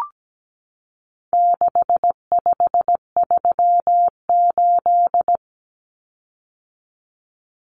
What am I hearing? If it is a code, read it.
Morse code, heard as a keyed tone, about 17 wpm, 700 Hz. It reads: 6538
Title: Common Cut Numbers - 4-Digit 17wpm